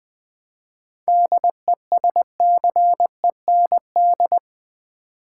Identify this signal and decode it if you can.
Morse code, heard as a keyed tone, about 20 wpm, 700 Hz. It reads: DESCEND
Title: 7 Letter Words - First 3 Letters Given - 20wpm